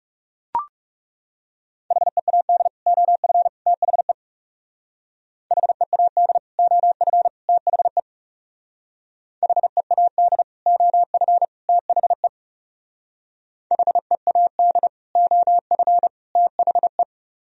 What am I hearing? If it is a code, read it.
Morse code, heard as a keyed tone, about 38 wpm, 700 Hz. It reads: HEAD OF THE HEAD OF THE HEAD OF THE HEAD OF THE